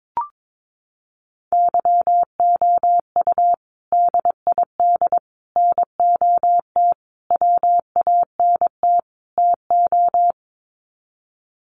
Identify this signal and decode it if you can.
Morse code, heard as a keyed tone, about 22 wpm, 700 Hz. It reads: YOU DID NOT WANT TO
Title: Sets of 5 Words Encore2 22wpm